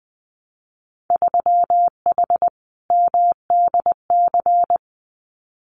Morse code, heard as a keyed tone, about 20 wpm, 700 Hz. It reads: 3H MDC
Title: Winter Field Day Exchanges Encore 20wpm